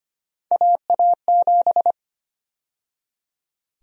Morse code, heard as a keyed tone, about 25 wpm, 700 Hz. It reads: AA7
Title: USA Call Sign Prefixes - 25wpm